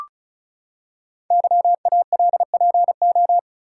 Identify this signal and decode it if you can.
Morse code, heard as a keyed tone, about 36 wpm, 690 Hz. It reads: YALPO
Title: Code Groups - 5 Letters - 35wpm